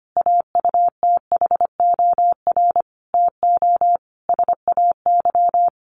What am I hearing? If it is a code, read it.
Morse code, heard as a keyed tone, about 25 wpm, 700 Hz. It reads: AUTHOR TO SAY
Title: Sentences from Top 1200 Words 25wpm